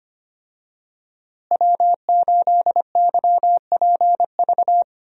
Morse code, heard as a keyed tone, about 25 wpm, 700 Hz. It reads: W8YPV
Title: USA Call Signs - 5 Characters - Repeated 2x - 25wpm